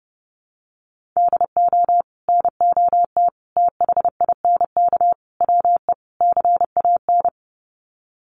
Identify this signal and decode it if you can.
Morse code, heard as a keyed tone, about 30 wpm, 700 Hz. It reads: DO NOT THINK WE CAN
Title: Sets of 5 Words Encore2 30wpm